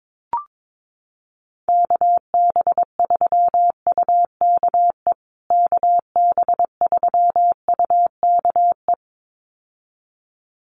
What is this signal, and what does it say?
Morse code, heard as a keyed tone, about 22 wpm, 700 Hz. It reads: KB3UKE KB3UKE
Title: USA Call Signs Repeated 2x 22wpm